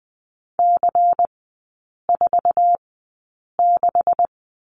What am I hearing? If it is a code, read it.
Morse code, heard as a keyed tone, about 20 wpm, spaced as for 8 wpm, 700 Hz. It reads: C46
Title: 127 Lesson 21e - Number 6 - Three Characters 20wpm